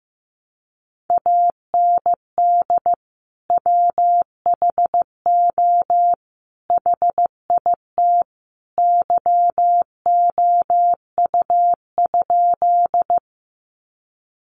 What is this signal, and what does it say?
Morse code, heard as a keyed tone, about 15 wpm, 700 Hz. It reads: AND WHO HIT YOU?
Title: Sentences from Top 1200 Words 15wpm